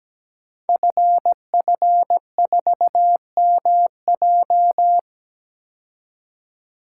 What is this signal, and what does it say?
Morse code, heard as a keyed tone, about 17 wpm, 700 Hz. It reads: FF4MJ
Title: Code Groups - 5 Letter:Numbers - 17wpm